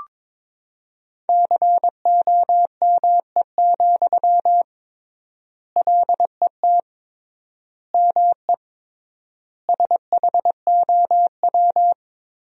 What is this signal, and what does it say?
Morse code, heard as a keyed tone, about 22 wpm, 700 Hz. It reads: COME, LET ME SHOW
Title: 3x Word-Spacing - Sentences from Top 500 Words - 22wpm